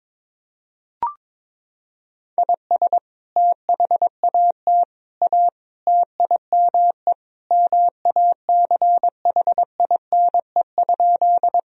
Morse code, heard as a keyed tone, about 22 wpm, 700 Hz. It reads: IS THAT A TIME MACHINE?
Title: Sentences from Top 1000 Words - Review Part 2 - 22wpm